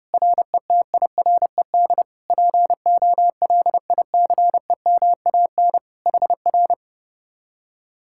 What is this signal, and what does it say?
Morse code, heard as a keyed tone, about 33 wpm, 700 Hz. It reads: RETIRED POLICEMAN HR